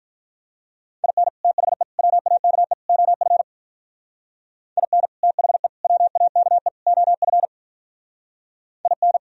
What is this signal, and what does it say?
Morse code, heard as a keyed tone, about 43 wpm, 700 Hz. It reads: IN THE WAKE OF IN THE WAKE OF IN
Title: Speed-Racing - Sets of 4 Words 35wpm